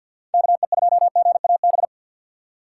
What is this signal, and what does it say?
Morse code, heard as a keyed tone, about 48 wpm, 700 Hz. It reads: KE2GAB
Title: USA Call Signs - 6 Characters - 4x Character Spacing - 50wpm